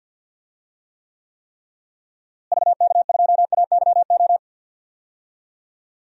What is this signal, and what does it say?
Morse code, heard as a keyed tone, about 46 wpm, 700 Hz. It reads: VK2AYO
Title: DXCC-100 Call Signs - 6 Characters - Repeated 3x - 50wpm